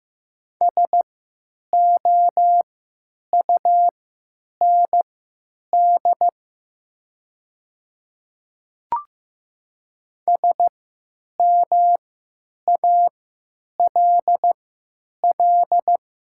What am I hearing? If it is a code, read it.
Morse code, heard as a keyed tone, about 15 wpm, spaced as for 9 wpm, 700 Hz. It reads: SOUND SMALL
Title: 3x Character Spacing - Top 200 Words - 15wpm